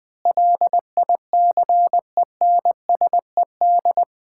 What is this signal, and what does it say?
Morse code, heard as a keyed tone, about 21 wpm, 700 Hz. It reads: LICENSED
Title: Common QSO Phrases - 4x Word-Spacing - 20wpm